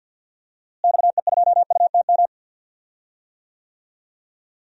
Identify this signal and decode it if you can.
Morse code, heard as a keyed tone, about 52 wpm, 700 Hz. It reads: XE2UTM
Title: DXCC-100 Call Signs - 6 Characters - Repeated 2x - 50wpm